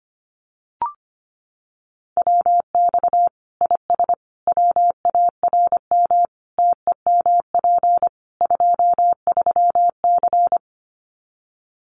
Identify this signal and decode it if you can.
Morse code, heard as a keyed tone, about 25 wpm, 700 Hz. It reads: WX IS WARM TEMP 23C
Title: Common QSO Phrases 25wpm